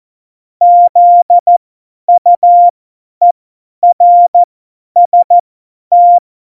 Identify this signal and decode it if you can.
Morse code, heard as a keyed tone, about 14 wpm, spaced as for 10 wpm, 700 Hz. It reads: ZUERST
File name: Morsekurs / Witze / Witz 24. November 2020, 50 F70